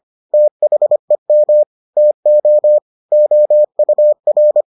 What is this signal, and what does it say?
Morse code, heard as a keyed tone, about 25 wpm, 600 Hz. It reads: THEM TO OUR